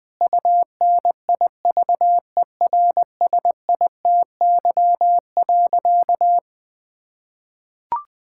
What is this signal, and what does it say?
Morse code, heard as a keyed tone, about 21 wpm, 700 Hz. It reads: UNIVERSITY.
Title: Sentences from Top 1000 Words - Review Part 5 - 20wpm